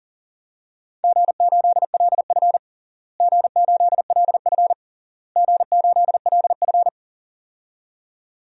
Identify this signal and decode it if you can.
Morse code, heard as a keyed tone, about 37 wpm, 700 Hz. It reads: G8LF G8LF G8LF
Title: DXCC-100 Call Signs - 4 Characters - Repeated 3x - 40wpm